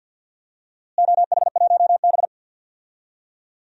Morse code, heard as a keyed tone, about 47 wpm, 700 Hz. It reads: OH1B
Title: DXCC-100 Call Signs - 4 Characters - Repeated 3x - 50wpm